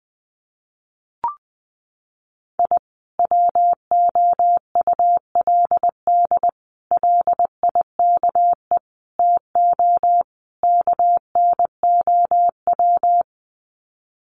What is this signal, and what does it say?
Morse code, heard as a keyed tone, about 20 wpm, 700 Hz. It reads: I WOULD LIKE TO KNOW
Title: Sets of 5 Words Encore 20wpm